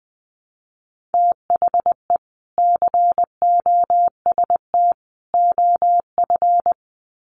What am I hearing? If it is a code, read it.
Morse code, heard as a keyed tone, about 20 wpm, 700 Hz. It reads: THE COST OF
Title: Sets of 3 Words 20wpm